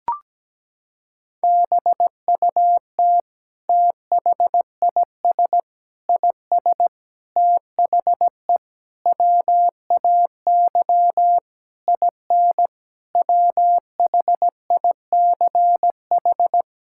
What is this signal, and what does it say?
Morse code, heard as a keyed tone, about 17 wpm, 700 Hz. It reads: BUT THIS IS THE WAY IN WHICH
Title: Sentences from Top 1000 Words - Review Part 5 - 17wpm